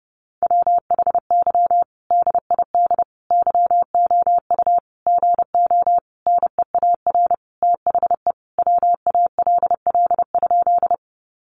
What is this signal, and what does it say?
Morse code, heard as a keyed tone, about 30 wpm, 700 Hz. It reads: WHY DID YOU GO NEAR THE WALL?
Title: Sentences from Top 1100 Words 30wpm